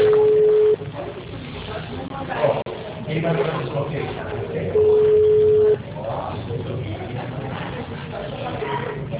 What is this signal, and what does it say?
Ringback tone (Europe, 425 Hz), with speech.
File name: out- -2008- - - .28965